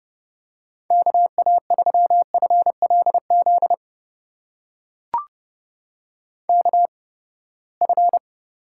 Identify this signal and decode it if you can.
Morse code, heard as a keyed tone, about 30 wpm, spaced as for 21 wpm, 700 Hz. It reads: KA3FLZ KF
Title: USA Call Signs - 8x Character Spacing - 30wpm